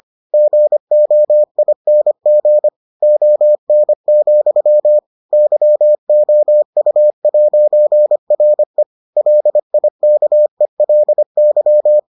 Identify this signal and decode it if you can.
Morse code, heard as a keyed tone, about 25 wpm, 600 Hz. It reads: GOING ON, YOU'RE LIKELY